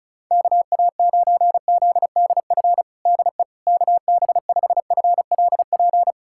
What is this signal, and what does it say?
Morse code, heard as a keyed tone, about 36 wpm, 700 Hz. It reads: KA9ZDF DE KB5FLP
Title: USA Callsign DE Callsign - 6 Characters - 35wpm